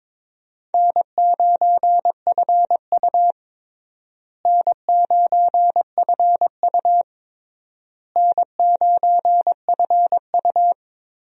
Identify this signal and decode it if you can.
Morse code, heard as a keyed tone, about 22 wpm, 700 Hz. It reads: N9FU N9FU N9FU
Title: USA Call Signs - 4 Characters - Repeated 3x - 22wpm